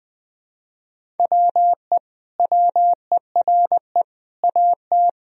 Morse code, heard as a keyed tone, about 20 wpm, 700 Hz. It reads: WE WERE AT
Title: Sentences from Top 1000 Words - Review Part 1 - 20wpm